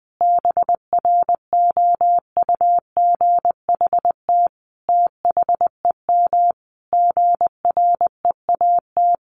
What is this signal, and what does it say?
Morse code, heard as a keyed tone, about 20 wpm, 700 Hz. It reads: BROUGHT THEM GREAT
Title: Sentences from Top 1300 Words 20wpm